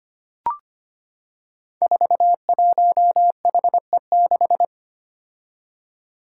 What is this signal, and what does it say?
Morse code, heard as a keyed tone, about 25 wpm, 700 Hz. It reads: 41HE6
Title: Five Challenging Letter-Number Combinations 25wpm